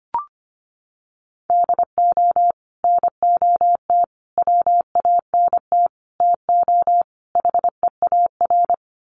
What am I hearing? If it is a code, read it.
Morse code, heard as a keyed tone, about 25 wpm, 700 Hz. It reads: DO NOT WANT TO HEAR
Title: Sets of 5 Words 25wpm